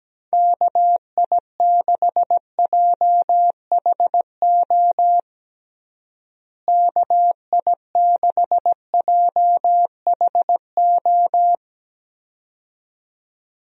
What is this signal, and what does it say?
Morse code, heard as a keyed tone, about 17 wpm, 700 Hz. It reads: KI6JHO KI6JHO
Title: USA Call Signs - 6 Characters - Repeated 2x - 17wpm